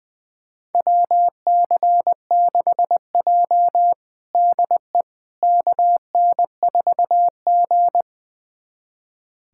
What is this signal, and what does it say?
Morse code, heard as a keyed tone, about 20 wpm, 700 Hz. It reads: WC6J DE KN4G
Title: USA Callsign DE Callsign - 4 Characters - 20wpm